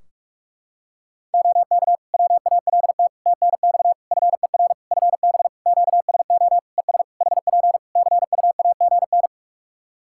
Morse code, heard as a keyed tone, about 44 wpm, 700 Hz. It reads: OK WALT TNX FER FB QSO ES HP CUAGN
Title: Practice QSO - Exchange 4 - Ending - 45wpm